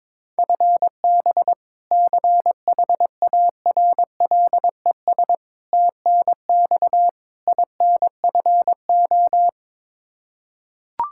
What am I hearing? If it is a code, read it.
Morse code, heard as a keyed tone, about 22 wpm, 700 Hz. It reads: FB CHARLES TNX INFO
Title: Practice QSO - Exchange 3 - More Info - 22wpm